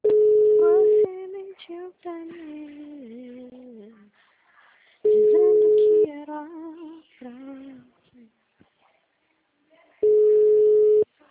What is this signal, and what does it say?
Ringback tone (Europe, 425 Hz), with speech.